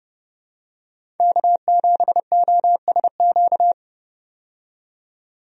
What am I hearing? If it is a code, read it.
Morse code, heard as a keyed tone, about 30 wpm, 700 Hz. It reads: K7OSQ
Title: USA Call Signs - 5 Characters - 30wpm